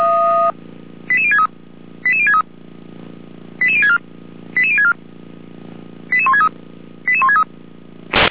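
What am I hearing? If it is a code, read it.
Five-tone selective call (ZVEI), heard as quick runs of five tones, tones of about 70 ms. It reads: 88063 88063 88064 88064 88163 88163
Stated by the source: ZVEI five-tone selective call received with an SDR.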